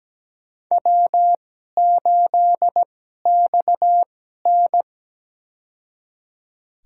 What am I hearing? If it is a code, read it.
Morse code, heard as a keyed tone, about 17 wpm, spaced as for 12 wpm, 700 Hz. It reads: W8XN
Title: USA Call Signs - 4 Characters - 2x Character Spacing - 17wpm